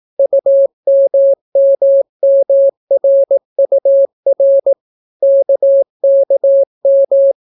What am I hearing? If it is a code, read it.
Morse code, heard as a keyed tone, about 18 wpm, 550 Hz. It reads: UMMMRUR KKM